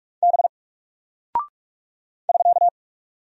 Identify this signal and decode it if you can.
Morse code, heard as a keyed tone, about 43 wpm, 700 Hz. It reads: B <SK>
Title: Single Character (Letter:Number:Punctuation:Prosign) Alt 45wpm